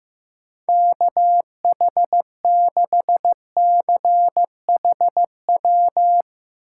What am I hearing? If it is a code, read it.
Morse code, heard as a keyed tone, about 15 wpm, 700 Hz. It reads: KH6CHW
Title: USA Call Signs - 6 Characters - 8x Character Spacing - 15wpm